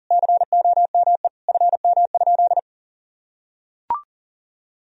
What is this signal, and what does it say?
Morse code, heard as a keyed tone, about 40 wpm, 700 Hz. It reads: COME FM?
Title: Sentences from Top 300 Words and Abbreviations 40wpm